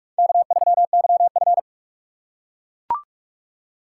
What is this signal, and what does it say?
Morse code, heard as a keyed tone, about 43 wpm, 700 Hz. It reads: K3YF